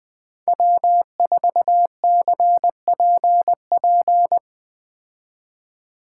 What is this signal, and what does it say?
Morse code, heard as a keyed tone, about 20 wpm, 700 Hz. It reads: W4CPP